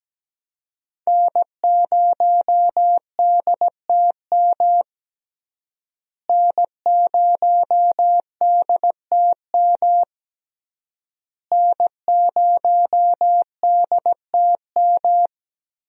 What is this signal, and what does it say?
Morse code, heard as a keyed tone, about 17 wpm, 700 Hz. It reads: N0DTM N0DTM N0DTM
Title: USA Call Signs - 5 Characters - Repeated 3x - 17wpm